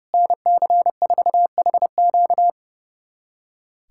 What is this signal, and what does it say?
Morse code, heard as a keyed tone, about 30 wpm, 700 Hz. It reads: NC4HQ
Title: USA Callsign DE Callsign - 5 Characters - 30wpm